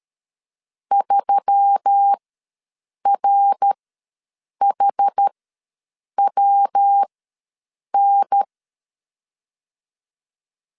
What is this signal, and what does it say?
Morse code, heard as a keyed tone, about 13 wpm, spaced as for 7 wpm, 800 Hz. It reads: 3RHWN
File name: LEZIONI / 9 2867 ws 12wpm 3431